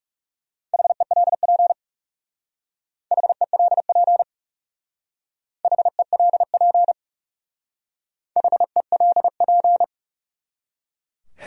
Morse code, heard as a keyed tone, about 35 wpm, 700 Hz. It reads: HELP HELP HELP HELP
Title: Speed-Racing Top 500 Words 30wpm